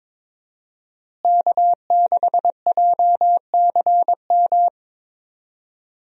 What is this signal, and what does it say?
Morse code, heard as a keyed tone, about 22 wpm, 700 Hz. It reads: K6JCM
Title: USA Call Signs - 5 Characters - Repeated 2x - 22wpm